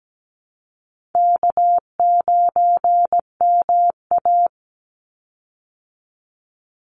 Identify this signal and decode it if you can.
Morse code, heard as a keyed tone, about 17 wpm, 700 Hz. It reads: K9MA
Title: Easy Call Signs Repeated 3x - 17wpm